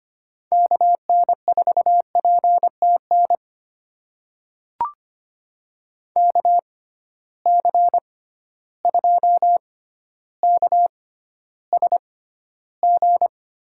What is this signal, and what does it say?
Morse code, heard as a keyed tone, about 25 wpm, spaced as for 13 wpm, 700 Hz. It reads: KN4PTN KC2KSG
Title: USA Call Signs - 6 Characters - 6x Character Spacing - 25wpm